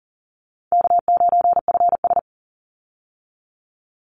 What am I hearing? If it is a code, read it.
Morse code, heard as a keyed tone, about 40 wpm, 700 Hz. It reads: K9FS
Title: Easy Call Signs Repeated 2x - 40wpm